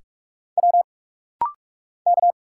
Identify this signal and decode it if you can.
Morse code, heard as a keyed tone, about 46 wpm, 690 Hz. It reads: W K
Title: Single Letters - Rapid-Fire 45wpm